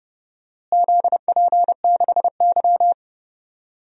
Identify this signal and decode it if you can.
Morse code, heard as a keyed tone, about 32 wpm, 700 Hz. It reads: ZP6Y
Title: DXCC-100 Call Signs - 4 Characters - Repeated 2x - 30wpm